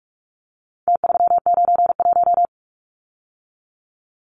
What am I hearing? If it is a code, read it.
Morse code, heard as a keyed tone, about 44 wpm, 700 Hz. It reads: T391
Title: Numbers - Contest Style - 4-Digit 45wpm